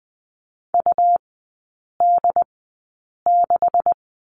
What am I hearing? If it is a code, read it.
Morse code, heard as a keyed tone, about 20 wpm, spaced as for 8 wpm, 700 Hz. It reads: UD6
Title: 212 Lesson 34e - Number 8 - Three Characters 20wpm